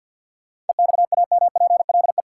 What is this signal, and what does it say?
Morse code, heard as a keyed tone, about 49 wpm, 700 Hz. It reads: EXAMPLE
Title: Sentences from Top 400 Words - New Word Drill - 50wpm